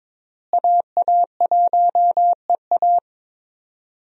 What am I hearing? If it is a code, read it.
Morse code, heard as a keyed tone, about 21 wpm, 700 Hz. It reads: AA1EA
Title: USA Call Signs - 5 Characters - Repeated 3x - 22wpm